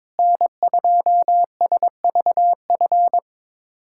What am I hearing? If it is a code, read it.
Morse code, heard as a keyed tone, about 21 wpm, 700 Hz. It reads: N2SVF
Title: USA Callsign DE Callsign 22wpm